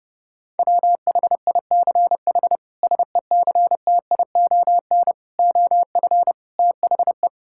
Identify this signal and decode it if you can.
Morse code, heard as a keyed tone, about 32 wpm, 700 Hz. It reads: WHICH SECTION OF THE
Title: Sentences from Top 1000 Words - Review Part 5 - 30wpm